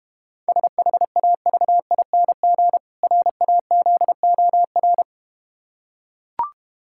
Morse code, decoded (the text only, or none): SHAVING RAZOR